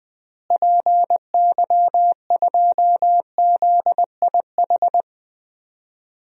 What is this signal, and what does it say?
Morse code, heard as a keyed tone, about 20 wpm, 700 Hz. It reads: PY2ZIH
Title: DXCC-100 Call Signs - 6 Characters - Repeated 3x - 20wpm